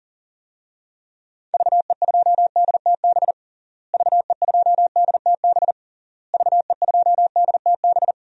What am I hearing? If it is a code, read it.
Morse code, heard as a keyed tone, about 40 wpm, 700 Hz. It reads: VE2DTB VE2DTB VE2DTB
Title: DXCC-100 Call Signs - 6 Characters - Repeated 3x - 40wpm